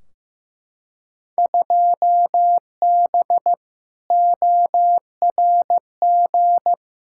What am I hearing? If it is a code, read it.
Morse code, heard as a keyed tone, about 15 wpm, 700 Hz. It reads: 2B ORG
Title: Field Day Exchanges Encore - 15wpm